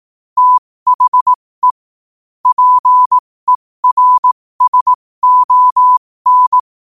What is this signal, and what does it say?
Morse code, heard as a keyed tone, about 18 wpm, spaced as for 15 wpm, 1000 Hz. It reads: THE PERSON